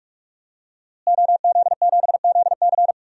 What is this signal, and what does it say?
Morse code, heard as a keyed tone, about 46 wpm, 700 Hz. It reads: OZ7ZC
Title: DXCC-100 Call Signs - 5 Characters - Repeated 2x - 45wpm